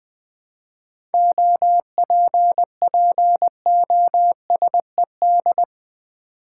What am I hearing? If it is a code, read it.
Morse code, heard as a keyed tone, about 20 wpm, 700 Hz. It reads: OPPOSED
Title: 7 Letter Words - First 3 Letters Given - 20wpm